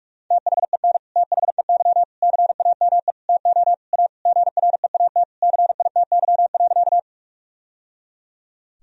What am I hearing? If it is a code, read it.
Morse code, heard as a keyed tone, about 41 wpm, 700 Hz. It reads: THEN THEY CAME TO A GREAT CITY.
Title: Sentences from Top 1000 Words - Review Part 1 - 45wpm